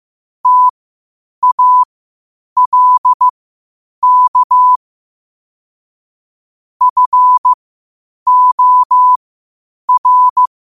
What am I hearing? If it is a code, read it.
Morse code, heard as a keyed tone, about 15 wpm, spaced as for 8 wpm, 1000 Hz. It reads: TALK FOR
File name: qotdfiles / 8wpm